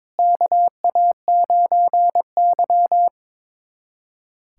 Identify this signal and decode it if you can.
Morse code, heard as a keyed tone, about 22 wpm, 700 Hz. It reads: KA9Y
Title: USA Callsign DE Callsign - 4 Characters - 22wpm